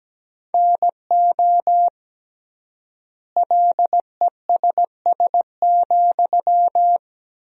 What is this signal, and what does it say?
Morse code, heard as a keyed tone, about 17 wpm, 700 Hz. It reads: NO LESS,
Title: 3x Word-Spacing - Sentences from Top 400 Words - 17wpm